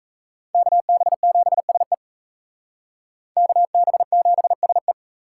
Morse code, heard as a keyed tone, about 36 wpm, 700 Hz. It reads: KB7SE KB7SE